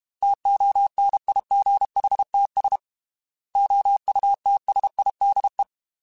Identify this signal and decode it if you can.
Morse code, heard as a keyed tone, about 32 wpm, 800 Hz. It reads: TONIGHTS OUTSIDE
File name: Weather Sets / Weather Sets-mp3s / Weather 4 10 @30